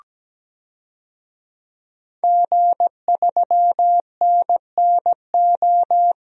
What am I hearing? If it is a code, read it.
Morse code, heard as a keyed tone, about 17 wpm, 700 Hz. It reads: G3NNO
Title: DXCC-100 Call Signs - 5 Characters - 17wpm